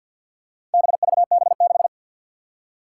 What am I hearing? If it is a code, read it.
Morse code, heard as a keyed tone, about 52 wpm, 700 Hz. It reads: BVB6
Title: Four Challenging Letter-Number Combinations 50wpm